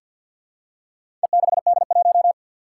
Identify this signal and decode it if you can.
Morse code, heard as a keyed tone, about 50 wpm, 700 Hz. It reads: E6D1
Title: Four Challenging Letter-Number Combinations 50wpm